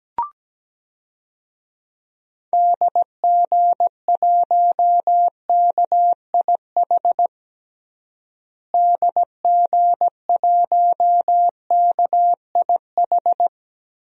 Morse code, heard as a keyed tone, about 17 wpm, 700 Hz. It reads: DG1KIH DG1KIH
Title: DXCC-100 Call Signs - 6 Characters - Repeated 3x - 17wpm